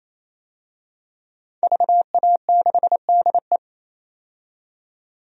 Morse code, heard as a keyed tone, about 28 wpm, 700 Hz. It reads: VA6DE
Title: DXCC-100 Call Signs - 5 Characters - 28wpm